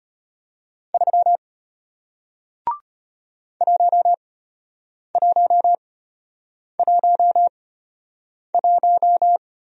Morse code, heard as a keyed tone, about 33 wpm, spaced as for 8 wpm, 700 Hz. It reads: 3 1111